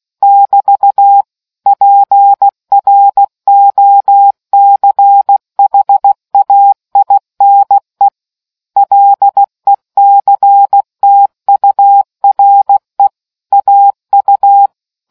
Morse code, heard as a keyed tone, about 16 wpm, 800 Hz. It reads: <BT> PROCHAINE LECTURE AU